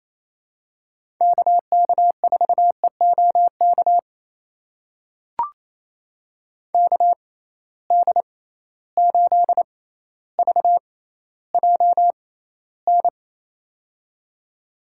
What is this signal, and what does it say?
Morse code, heard as a keyed tone, about 28 wpm, spaced as for 14 wpm, 700 Hz. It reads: KK4EOK KD8VJN
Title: USA Call Signs - 6 Characters - 6x Character Spacing - 28wpm